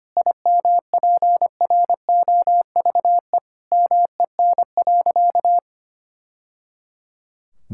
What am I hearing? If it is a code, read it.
Morse code, heard as a keyed tone, about 25 wpm, 700 Hz. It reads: IMPROVE MEN.
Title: Sentences from Top 1000 Words - Review Part 5 - 25wpm